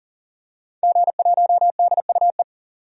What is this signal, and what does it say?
Morse code, heard as a keyed tone, about 40 wpm, 700 Hz. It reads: G1DUE
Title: DXCC-100 Call Signs - 5 Characters - 40wpm